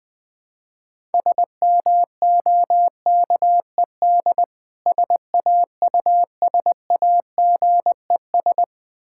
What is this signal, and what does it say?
Morse code, heard as a keyed tone, about 20 wpm, 690 Hz. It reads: SMOKED SAUSAGES
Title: Difficult Words with Hints - 20wpm